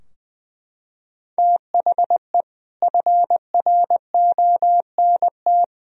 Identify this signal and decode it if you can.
Morse code, heard as a keyed tone, about 20 wpm, 700 Hz. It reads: THE FRONT
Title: Sentences from Top 1000 Words - Review Part 2 - 20wpm